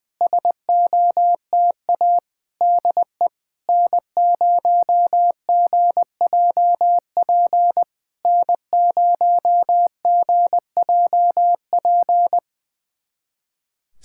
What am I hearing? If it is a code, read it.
Morse code, heard as a keyed tone, about 20 wpm, 700 Hz. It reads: SOTA DE N0GJP N0GJP
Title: SOTA QSO Practice 20wpm